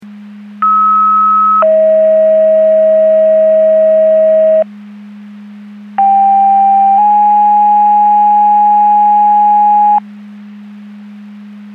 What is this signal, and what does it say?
Two-tone page, heard as a one-second tone, then a three-second tone, about 1290 then 640 Hz, about 790 then 830 Hz. It is heard from 0.6 to 10.0 s.
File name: Portage / 05797 17-58 Portage 154.385 Paging